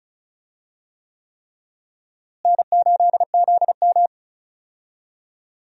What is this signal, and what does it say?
Morse code, heard as a keyed tone, about 36 wpm, 700 Hz. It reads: N8ZM
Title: USA Call Signs - 4 Characters - Repeated 3x - 35wpm